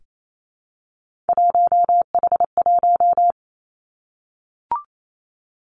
Morse code, heard as a keyed tone, about 28 wpm, 700 Hz. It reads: 1H1
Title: Three Challenging Letter-Number Combinations 28wpm